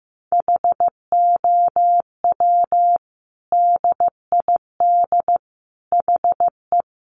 Morse code, heard as a keyed tone, about 15 wpm, 700 Hz. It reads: HOW DID HE
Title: Sentences from Top 1400 Words 15wpm